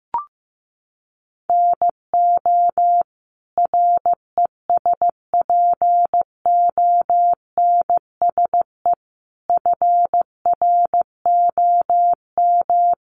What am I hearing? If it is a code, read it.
Morse code, heard as a keyed tone, about 14 wpm, 700 Hz. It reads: NO RESPONSE FROM
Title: Sentences from Top 1400 Words 15wpm